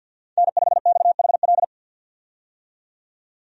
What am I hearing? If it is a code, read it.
Morse code, heard as a keyed tone, about 50 wpm, 700 Hz. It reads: N5XHL
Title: USA Call Signs - 5 Characters - 50wpm